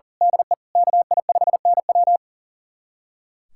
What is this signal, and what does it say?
Morse code, heard as a keyed tone, about 40 wpm, 700 Hz. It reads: DE KI5NW
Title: USA Callsign DE Callsign 40wpm